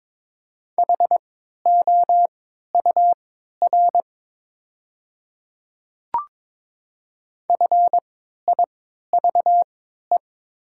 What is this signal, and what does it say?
Morse code, heard as a keyed tone, about 22 wpm, spaced as for 12 wpm, 700 Hz. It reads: HOUR FIVE
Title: 3x Character Spacing - Top 500 Words - 22wpm